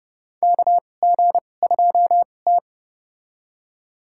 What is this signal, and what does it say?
Morse code, heard as a keyed tone, about 30 wpm, spaced as for 22 wpm, 700 Hz. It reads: KG2T
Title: USA Call Signs - 4 Characters - 2x Character Spacing - 30wpm